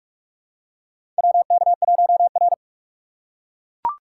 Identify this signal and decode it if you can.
Morse code, heard as a keyed tone, about 44 wpm, 700 Hz. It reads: WK1R
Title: USA Call Signs - 4 Characters - 2x Character Spacing - 45wpm